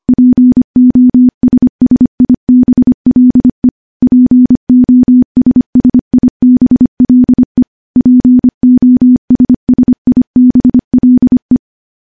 Morse code, heard as a keyed tone, about 25 wpm, 250 Hz. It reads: POSSIBLE POSSIBLE POSSIBLE
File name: morse-training-25WPM